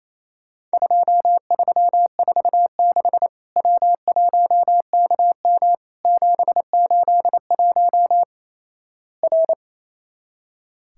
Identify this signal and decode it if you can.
Morse code, heard as a keyed tone, about 28 wpm, 700 Hz. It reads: 2346 W1KM 781 R
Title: Worked All Europe DX Contest - QTC Exchange - 28wpm